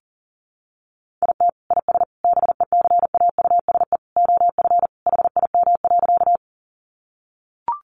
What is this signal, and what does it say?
Morse code, heard as a keyed tone, about 38 wpm, 700 Hz. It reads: IT IS BECAUSE OF HIM.
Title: Sentences from Top 600 Words 40wpm